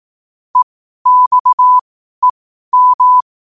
Morse code, heard as a keyed tone, about 18 wpm, spaced as for 12 wpm, 1000 Hz. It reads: EXEM
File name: qotdfiles / 12wpm